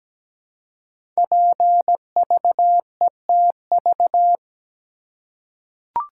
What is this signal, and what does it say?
Morse code, heard as a keyed tone, about 17 wpm, 700 Hz. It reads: PVETV
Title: Code Groups - 5 Letters - 17wpm